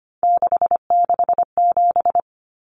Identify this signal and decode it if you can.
Morse code, heard as a keyed tone, about 25 wpm, 700 Hz. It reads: 667
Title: Common Cut Numbers - 3-Digit 25wpm